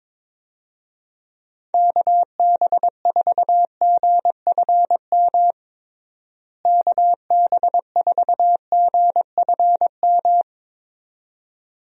Morse code, heard as a keyed tone, about 21 wpm, 700 Hz. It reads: KB4GFM KB4GFM
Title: USA Call Signs - 6 Characters - Repeated 2x - 22wpm